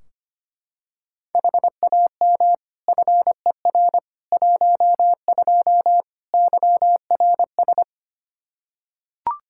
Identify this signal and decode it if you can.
Morse code, heard as a keyed tone, about 25 wpm, 700 Hz. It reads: HAM FER 12 YRS
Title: Practice QSO - Exchange 3 - More Info - 25wpm